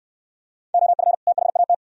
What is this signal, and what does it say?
Morse code, heard as a keyed tone, about 69 wpm, 690 Hz. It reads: OF THAT